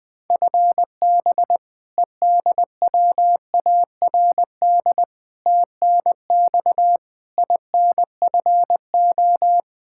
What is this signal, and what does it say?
Morse code, heard as a keyed tone, about 20 wpm, 700 Hz. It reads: FB EDWARD TNX INFO
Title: Practice QSO - Exchange 3 - More Info - 20wpm